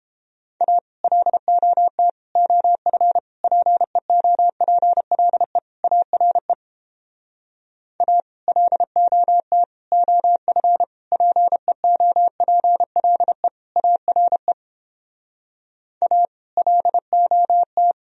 Morse code, heard as a keyed tone, about 29 wpm, 700 Hz. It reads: A LOT OF PEOPLE ARE A LOT OF PEOPLE ARE A LOT